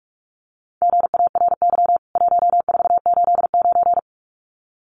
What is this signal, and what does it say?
Morse code, heard as a keyed tone, about 44 wpm, 700 Hz. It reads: GARY 1489
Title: CWT Exchange 45wpm